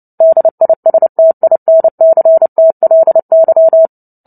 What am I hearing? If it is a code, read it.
Morse code, heard as a keyed tone, about 30 wpm, 650 Hz. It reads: DISTINCTLY
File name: Book Two / 03-Chapter 3 - 29 WPM